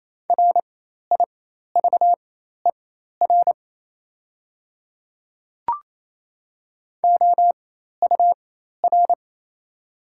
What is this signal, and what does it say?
Morse code, heard as a keyed tone, about 28 wpm, spaced as for 13 wpm, 690 Hz. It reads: RIVER OUR
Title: 4x Character Spacing - Top 300 Words - 28wpm